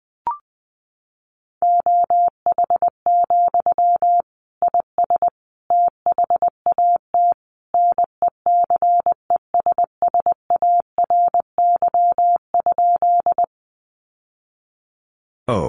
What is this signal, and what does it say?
Morse code, heard as a keyed tone, about 20 wpm, 700 Hz. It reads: OH, IS THAT NECESSARY?
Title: Sentences from Top 800 Words 20wpm